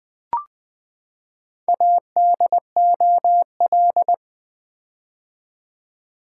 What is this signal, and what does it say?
Morse code, heard as a keyed tone, about 20 wpm, 700 Hz. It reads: ADOL
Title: Build A Word - 4 Letters At A Time - 10-Letter Words - 20wpm